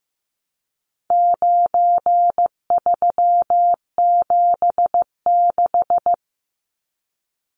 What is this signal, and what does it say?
Morse code, heard as a keyed tone, about 15 wpm, 700 Hz. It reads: 9376
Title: Numbers - 4-Digit 15wpm